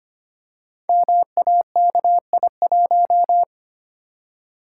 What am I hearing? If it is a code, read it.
Morse code, heard as a keyed tone, about 25 wpm, 700 Hz. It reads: MAKI1
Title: Code Groups - 5 Letter:Numbers - 25wpm